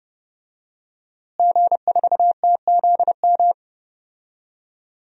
Morse code, heard as a keyed tone, about 30 wpm, 700 Hz. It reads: G4TZM
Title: DXCC-100 Call Signs - 5 Characters - Repeated 2x - 30wpm